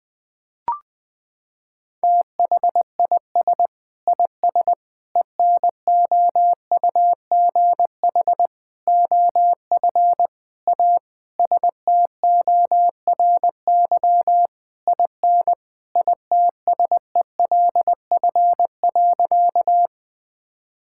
Morse code, heard as a keyed tone, about 21 wpm, 700 Hz. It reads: THIS IS ENOUGH OF A STORY IN ITSELF.
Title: Sentences from Top 1000 Words - Review Part 5 - 20wpm